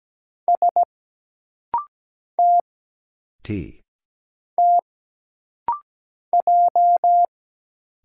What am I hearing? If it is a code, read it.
Morse code, heard as a keyed tone, about 17 wpm, 700 Hz. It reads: S T T J
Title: Single Letter-Number Alt - Rapid-Fire 17wpm